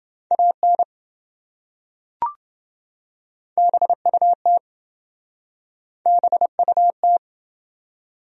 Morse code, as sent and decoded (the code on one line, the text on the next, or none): .- -. / -... ..- - / -... ..- -
AN BUT BUT